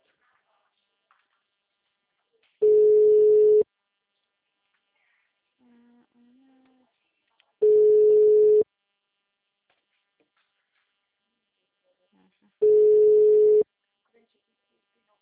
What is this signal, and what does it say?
Ringback tone (Europe, 425 Hz).